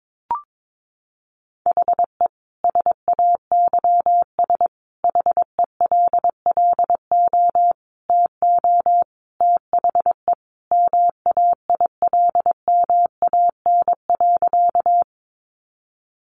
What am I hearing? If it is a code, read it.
Morse code, heard as a keyed tone, about 22 wpm, 700 Hz. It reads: HE SAYS HELLO TO THE MAILMAN.
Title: Easy Phrases 22wpm